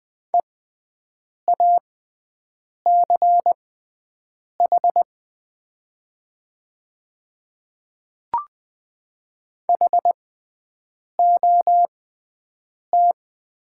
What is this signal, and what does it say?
Morse code, heard as a keyed tone, about 20 wpm, spaced as for 7 wpm, 700 Hz. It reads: EACH HOT
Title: 6x Character Spacing - Top 100 Words - 20wpm